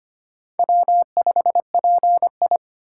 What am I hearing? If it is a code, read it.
Morse code, heard as a keyed tone, about 25 wpm, 700 Hz. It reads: W5PI
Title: USA Call Signs - 4 Characters - 2x Character Spacing - 25wpm